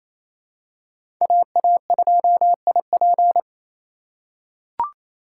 Morse code, heard as a keyed tone, about 27 wpm, 700 Hz. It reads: AA2IP
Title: USA Call Signs - 5 Characters - 2x Character Spacing - 28wpm